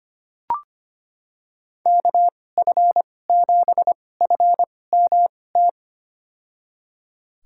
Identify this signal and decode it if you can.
Morse code, heard as a keyed tone, about 25 wpm, spaced as for 18 wpm, 700 Hz. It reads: KF7FMT
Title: USA Call Signs - 6 Characters - 2x Character Spacing - 25wpm